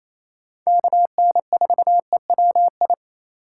Morse code, heard as a keyed tone, about 28 wpm, 700 Hz. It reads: KN4EWI
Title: K1USN SST QSO Practice - 28wpm